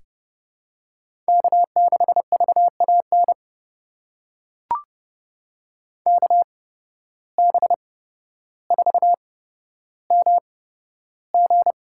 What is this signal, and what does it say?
Morse code, heard as a keyed tone, about 30 wpm, spaced as for 13 wpm, 700 Hz. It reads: K6VAN KB4MG
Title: USA Call Signs - 8x Character Spacing - 30wpm